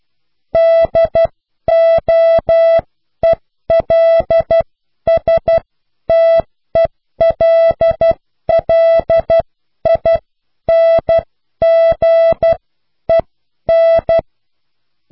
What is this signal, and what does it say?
Morse code, heard as a keyed tone, about 12 wpm, spaced as for 10 wpm, 660 Hz. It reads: DOELSTELLINGEN